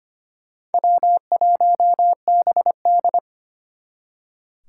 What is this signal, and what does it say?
Morse code, heard as a keyed tone, about 32 wpm, 700 Hz. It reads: W1BD